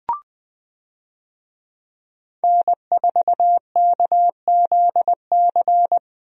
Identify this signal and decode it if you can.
Morse code, heard as a keyed tone, about 20 wpm, 700 Hz. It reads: N4KZC
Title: USA Call Signs - 5 Characters - Repeated 3x - 20wpm